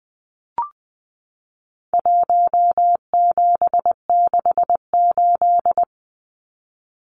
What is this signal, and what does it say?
Morse code, heard as a keyed tone, about 20 wpm, 700 Hz. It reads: 1768
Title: Numbers - Contest Style - 4-Digit 20wpm